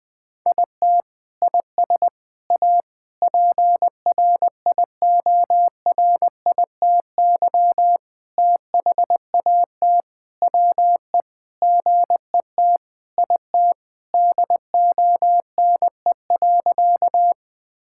Morse code, heard as a keyed tone, about 21 wpm, 700 Hz. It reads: IT IS A PRIORITY THAT WE GET IT DONE.
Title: Sentences from Top 1600 Words 20wpm